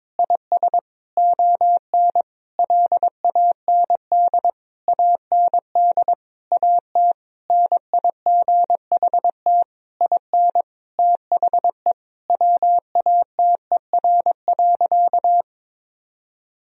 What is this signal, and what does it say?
Morse code, heard as a keyed tone, about 22 wpm, 700 Hz. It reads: IS ON LAND AND AT NIGHT IN THE WATER.
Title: Sentences from Top 1000 Words - Review Part 1 - 22wpm